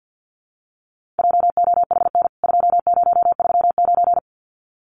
Morse code, heard as a keyed tone, about 50 wpm, 700 Hz. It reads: JOHN 2039
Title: CWT Exchange 50wpm